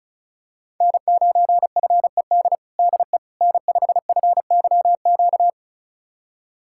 Morse code, heard as a keyed tone, about 36 wpm, 700 Hz. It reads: N9FED DE N5FYQ
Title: USA Callsign DE Callsign - 5 Characters - 35wpm